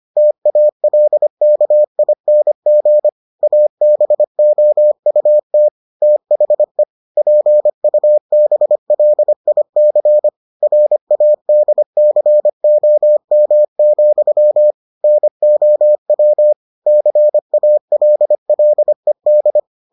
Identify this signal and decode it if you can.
Morse code, heard as a keyed tone, about 25 wpm, 600 Hz. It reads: TALKING ABOUT THE PUBLIC RADCOM, NOW CALLED